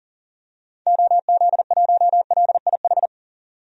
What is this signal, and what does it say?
Morse code, heard as a keyed tone, about 40 wpm, 700 Hz. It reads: OZ1LIH
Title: DXCC-100 Call Signs - 6 Characters - 40wpm